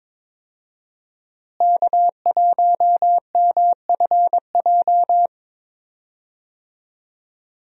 Morse code, heard as a keyed tone, about 22 wpm, 700 Hz. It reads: K1MFJ
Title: USA Call Signs - 5 Characters - Repeated 3x - 22wpm